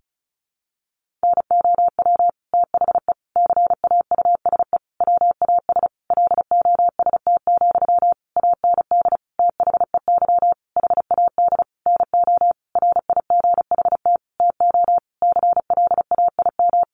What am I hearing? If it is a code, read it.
Morse code, heard as a keyed tone, about 34 wpm, 700 Hz. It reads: NOW THE CAUSE WAS LOST, AND THEY HAD NO RIGHT TO CLAIM
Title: Sentences from Top 1500 Words 35wpm